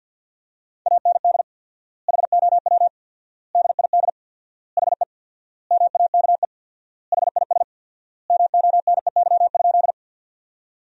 Morse code, heard as a keyed tone, about 51 wpm, 700 Hz. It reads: AND HOW DID HE MAKE HIS MONEY?
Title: 4x Word-Spacing - Sentences from Top 400 Words - 50wpm